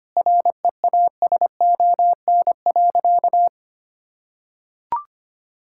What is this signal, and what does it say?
Morse code, heard as a keyed tone, about 25 wpm, 700 Hz. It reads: REASON.
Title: Sentences from Top 1000 Words - Review Part 6 - 25wpm